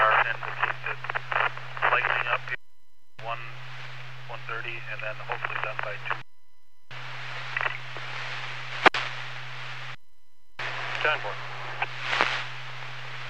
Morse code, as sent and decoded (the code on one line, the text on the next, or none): none